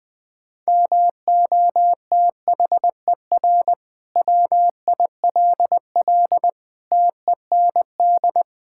Morse code, heard as a keyed tone, about 21 wpm, 700 Hz. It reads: MOTHER WILL TEND
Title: Sentences from Top 1000 Words - Review Part 5 - 20wpm